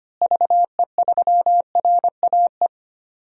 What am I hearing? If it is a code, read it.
Morse code, heard as a keyed tone, about 25 wpm, 700 Hz. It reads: VE3RAE
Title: DXCC-100 Call Signs - 6 Characters - Repeated 2x - 25wpm